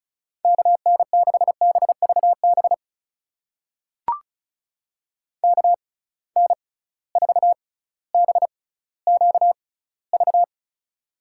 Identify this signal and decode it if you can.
Morse code, heard as a keyed tone, about 36 wpm, spaced as for 18 wpm, 700 Hz. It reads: KN6BVB KN4BQV
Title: USA Call Signs - 6x Character Spacing - 35wpm